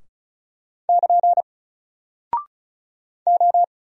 Morse code, heard as a keyed tone, about 36 wpm, 700 Hz. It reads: <KN> O
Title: Single Character (Letter:Number:Punctuation:Prosign) 35wpm